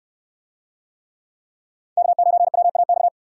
Morse code, heard as a keyed tone, about 68 wpm, 700 Hz. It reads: K9WAX